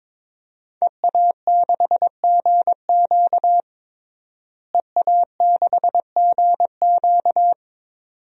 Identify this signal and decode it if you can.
Morse code, heard as a keyed tone, about 21 wpm, 700 Hz. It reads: EA6GQ EA6GQ
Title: DXCC-100 Call Signs - 5 Characters - Repeated 2x - 22wpm